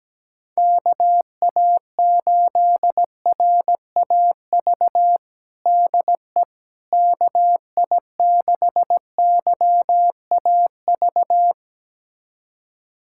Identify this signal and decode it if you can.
Morse code, heard as a keyed tone, about 17 wpm, 700 Hz. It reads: KA8RAV DE KI6YAV
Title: USA Callsign DE Callsign - 6 Characters - 17wpm